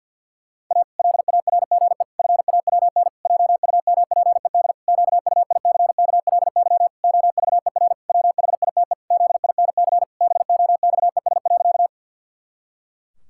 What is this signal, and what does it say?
Morse code, heard as a keyed tone, about 53 wpm, 700 Hz. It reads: A LARGE FAWN JUMPED QUICKLY OVER WHITE ZINC BOXES.
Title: Pangram Challenge v2 - 50wpm